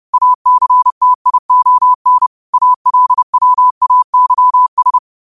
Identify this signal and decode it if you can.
Morse code, heard as a keyed tone, about 30 wpm, 1000 Hz. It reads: ACTION ALWAYS
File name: qotdfiles / 30wpm